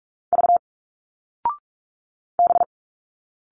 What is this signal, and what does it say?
Morse code, heard as a keyed tone, about 45 wpm, 700 Hz. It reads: V B
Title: Single Letters 45wpm